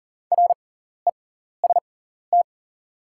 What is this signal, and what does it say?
Morse code, heard as a keyed tone, about 40 wpm, spaced as for 14 wpm, 690 Hz. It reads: REST
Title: 6x Character Spacing - Top 500 Words - 40wpm